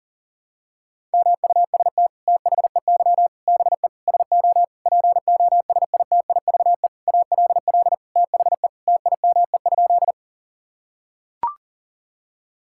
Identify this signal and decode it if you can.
Morse code, heard as a keyed tone, about 37 wpm, 700 Hz. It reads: MUST THEY BE SO POSITIVE ALL THE TIME?
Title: Sentences from Top 1000 Words - Review Part 5 - 40wpm